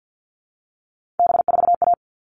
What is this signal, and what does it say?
Morse code, heard as a keyed tone, about 50 wpm, 700 Hz. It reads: B4A
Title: All Cut Numbers - 3-Digit 50wpm